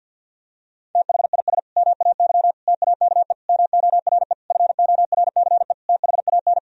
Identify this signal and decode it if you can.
Morse code, heard as a keyed tone, about 50 wpm, 700 Hz. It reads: THIS MAY TAKE MORE FORCE THAN
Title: Sentences from Top 1000 Words - Review Part 3 - 50wpm